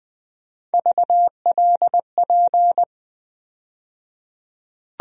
Morse code, heard as a keyed tone, about 20 wpm, 700 Hz. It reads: VLP
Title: USA Call Sign Postfixes - 3 Letters 20wpm